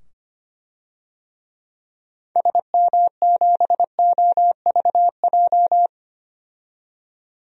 Morse code, heard as a keyed tone, about 25 wpm, 700 Hz. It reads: SM7OVJ
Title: DXCC-100 Call Signs - 6 Characters - 25wpm